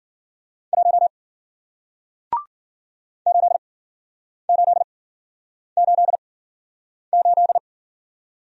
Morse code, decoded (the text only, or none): 1 8888